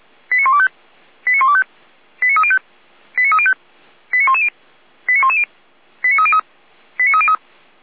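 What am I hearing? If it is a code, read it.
Five-tone selective call (ZVEI), heard as quick runs of five tones, tones of about 70 ms. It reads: 89125 89125 89305 89305 89110 89110 89303 89303
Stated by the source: ZVEI five-tone selective call code 89265, received by SDR.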